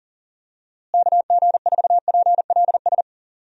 Morse code, heard as a keyed tone, about 40 wpm, 700 Hz. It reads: KG4PLS
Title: USA Call Signs - 6 Characters - Repeated 3x - 40wpm